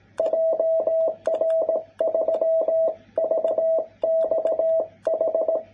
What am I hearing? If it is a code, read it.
Morse code, heard as a keyed tone, about 18 wpm, 660 Hz. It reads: JR3VXH
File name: jh8cbh / 2021cw